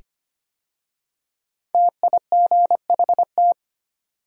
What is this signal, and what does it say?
Morse code, heard as a keyed tone, about 25 wpm, 700 Hz. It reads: TIGHT
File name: 5 Letter Words - First 3 Letters Given - 25wpm